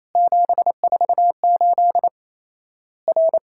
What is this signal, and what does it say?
Morse code, heard as a keyed tone, about 28 wpm, 700 Hz. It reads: 748 R
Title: Worked All Europe DX Contest - QTC Exchange - 28wpm